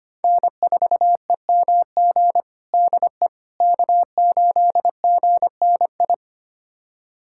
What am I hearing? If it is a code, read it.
Morse code, heard as a keyed tone, about 25 wpm, 700 Hz. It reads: N4EMG DE K8GNI
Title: USA Callsign DE Callsign - 5 Characters - 25wpm